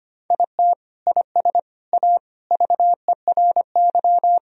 Morse code, heard as a keyed tone, about 25 wpm, 700 Hz. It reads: IT IS A VERY
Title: Sentences from Top 1000 Words - Review Part 4 - 25wpm